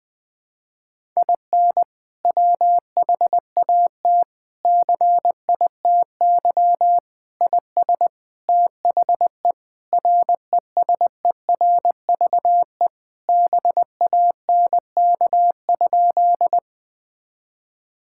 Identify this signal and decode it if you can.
Morse code, heard as a keyed tone, about 20 wpm, 700 Hz. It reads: IN WHAT CITY IS THE RESERVE BANK?
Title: Sentences from Top 1000 Words - Review Part 6 - 20wpm